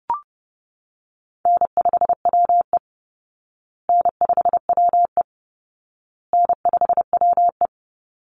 Morse code, heard as a keyed tone, about 30 wpm, 700 Hz. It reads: N5WE N5WE N5WE
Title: Easy Call Signs Repeated 3x - 30wpm